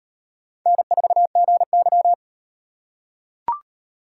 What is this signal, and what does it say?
Morse code, heard as a keyed tone, about 38 wpm, 700 Hz. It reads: N4GY